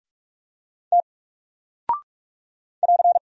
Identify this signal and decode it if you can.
Morse code, heard as a keyed tone, about 48 wpm, 700 Hz. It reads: T <AR>
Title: Single Character (Letter:Number:Punctuation:Prosign) Alt 45wpm